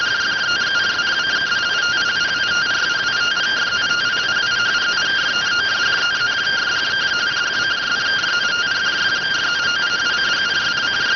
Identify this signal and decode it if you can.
Two-tone teleprinter signal, about 170 Hz shift.